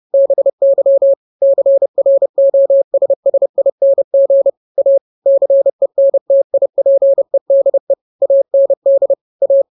Morse code, heard as a keyed tone, about 30 wpm, 550 Hz. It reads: BY CROSSING A CENTIPEDE AND A